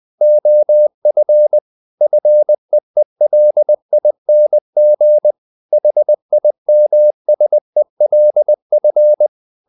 Morse code, heard as a keyed tone, about 20 wpm, 600 Hz. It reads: OF FEELING HIMSELF
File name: the old man and the sea part 2 20wpm